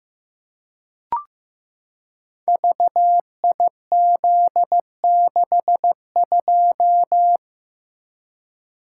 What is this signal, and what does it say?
Morse code, heard as a keyed tone, about 15 wpm, 700 Hz. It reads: VIZ62
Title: Code Groups - 5 Letter:Numbers - 15wpm